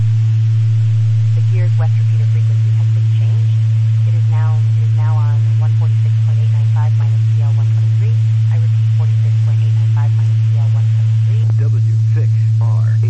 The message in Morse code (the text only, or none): none